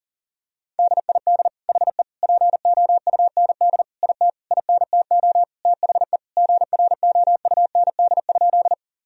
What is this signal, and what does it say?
Morse code, heard as a keyed tone, about 40 wpm, 700 Hz. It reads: DID HE POUND IT INTO THE GROUND?
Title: Sentences from Top 1000 Words - Review Part 2 - 40wpm